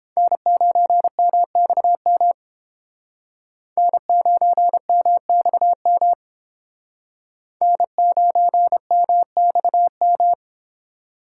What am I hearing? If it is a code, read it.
Morse code, heard as a keyed tone, about 29 wpm, 700 Hz. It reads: N9MXM N9MXM N9MXM